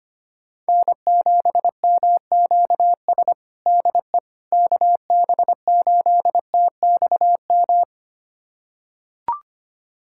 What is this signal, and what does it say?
Morse code, heard as a keyed tone, about 25 wpm, 700 Hz. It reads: N7MQS DE KB8TXM
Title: USA Callsign DE Callsign 25wpm